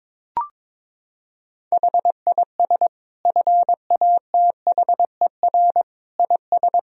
Morse code, heard as a keyed tone, about 22 wpm, 700 Hz. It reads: HIS FATHER IS
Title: Sentences from Top 1000 Words - Review Part 2 - 22wpm